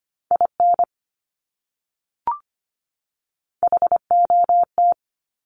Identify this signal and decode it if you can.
Morse code, heard as a keyed tone, about 25 wpm, 700 Hz. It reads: IN HOT
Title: Top 500 Words 25wpm